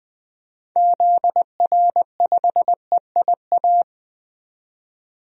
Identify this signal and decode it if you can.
Morse code, heard as a keyed tone, about 20 wpm, 700 Hz. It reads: ZR5EIA
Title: DXCC-100 Call Signs - 6 Characters - 20wpm